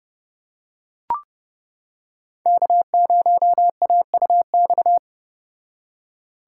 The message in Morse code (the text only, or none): K0AUX